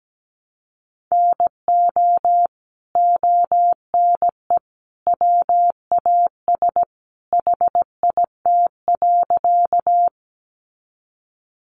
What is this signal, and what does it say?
Morse code, heard as a keyed tone, about 17 wpm, 700 Hz. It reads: NO ONE WAS HIT.
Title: Sentences from Top 1200 Words 17wpm